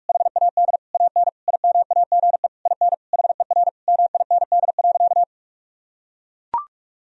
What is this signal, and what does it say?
Morse code, heard as a keyed tone, about 42 wpm, 700 Hz. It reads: HAD AN IMAGE IN HER MIND.
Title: Sentences from Top 1000 Words - Review Part 4 - 45wpm